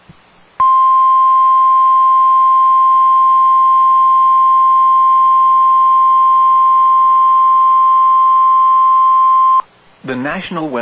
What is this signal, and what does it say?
Emergency Alert System (SAME) tones (the 1050 Hz attention tone), with speech. It is heard from 0.6 to 9.7 s.